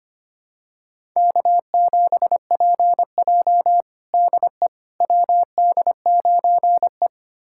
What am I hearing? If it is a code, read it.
Morse code, heard as a keyed tone, about 25 wpm, 700 Hz. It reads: K7PJ DE WD9E
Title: USA Callsign DE Callsign - 4 Characters - 25wpm